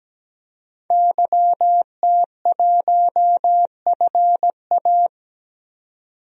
Morse code, heard as a keyed tone, about 17 wpm, 700 Hz. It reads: YT1FA
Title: DXCC-100 Call Signs - 5 Characters - 17wpm